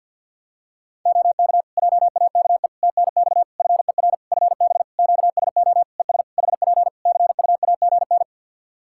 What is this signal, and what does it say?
Morse code, heard as a keyed tone, about 49 wpm, 700 Hz. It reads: OK JAKE TNX FER FB QSO ES HP CUAGN
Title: Practice QSO - Exchange 4 - Ending - 50wpm